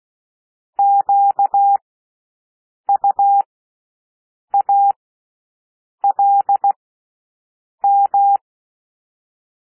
Morse code, heard as a keyed tone, about 16 wpm, spaced as for 6 wpm, 800 Hz. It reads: QUALM
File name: cwmp3 / W1 SOUND WORDS 16 wpm